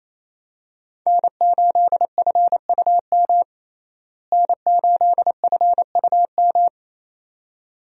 Morse code, heard as a keyed tone, about 28 wpm, 700 Hz. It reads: N8FUM N8FUM
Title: USA Call Signs - 5 Characters - Repeated 2x - 28wpm